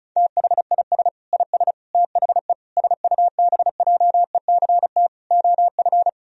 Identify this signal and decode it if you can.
Morse code, heard as a keyed tone, about 36 wpm, 700 Hz. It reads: THIS IS THE SUBJECT OF
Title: Sentences from Top 1000 Words - Review Part 4 - 35wpm